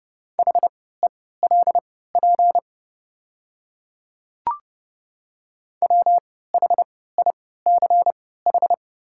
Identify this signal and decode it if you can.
Morse code, heard as a keyed tone, about 30 wpm, spaced as for 17 wpm, 700 Hz. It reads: HELP WHICH
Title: 3x Character Spacing - Top 200 Words - 30wpm